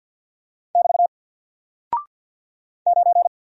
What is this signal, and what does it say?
Morse code, heard as a keyed tone, about 48 wpm, 700 Hz. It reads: <BT> 9
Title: Single Character (Letter:Number:Punctuation:Prosign) 50wpm